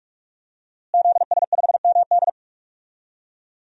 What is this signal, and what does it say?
Morse code, heard as a keyed tone, about 41 wpm, 700 Hz. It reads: ZS5MD
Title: DXCC-100 Call Signs - 5 Characters - Repeated 2x - 45wpm